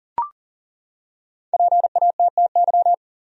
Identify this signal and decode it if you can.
Morse code, heard as a keyed tone, about 40 wpm, 700 Hz. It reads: PATTY
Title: USA Common Names - Max 5 Letters - 40wpm